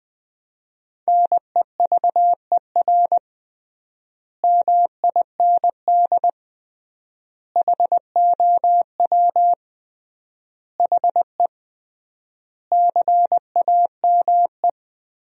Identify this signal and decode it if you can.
Morse code, heard as a keyed tone, about 20 wpm, 700 Hz. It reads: NEVER MIND HOW HE CAME
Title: 3x Word-Spacing - Sentences from Top 500 Words - 20wpm